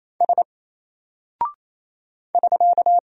Morse code, heard as a keyed tone, about 27 wpm, 700 Hz. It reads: S <SK>
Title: Single Character (Letter:Number:Punctuation:Prosign) 28wpm